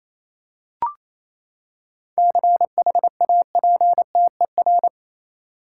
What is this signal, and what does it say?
Morse code, heard as a keyed tone, about 28 wpm, 700 Hz. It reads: CHAPTER
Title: Seven Letter Words - 28wpm